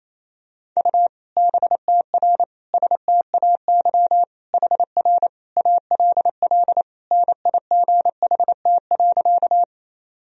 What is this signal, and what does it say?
Morse code, heard as a keyed tone, about 28 wpm, 700 Hz. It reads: U BTR STAY HR ALL NIGHT.
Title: Sentences from Top 500 Words and Abbreviations 28wpm